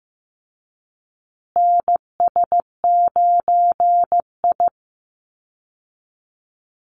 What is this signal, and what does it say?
Morse code, heard as a keyed tone, about 15 wpm, 700 Hz. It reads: NS9I
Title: Call Signs Repeated 3x - 15wpm